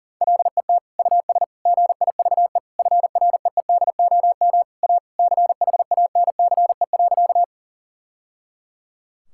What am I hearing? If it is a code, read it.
Morse code, heard as a keyed tone, about 40 wpm, 700 Hz. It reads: LET US GIVE FREEDOM A CHANCE.
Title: Sentences from Top 1000 Words - Review Part 5 - 40wpm